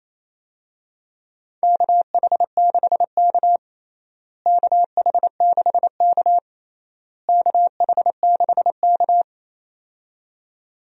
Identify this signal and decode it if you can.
Morse code, heard as a keyed tone, about 28 wpm, 700 Hz. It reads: KH6K KH6K KH6K
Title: DXCC-100 Call Signs - 4 Characters - Repeated 3x - 28wpm